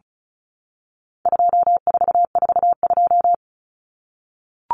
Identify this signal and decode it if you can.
Morse code, heard as a keyed tone, about 34 wpm, 700 Hz. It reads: 2442
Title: Numbers - 4-Digit 35wpm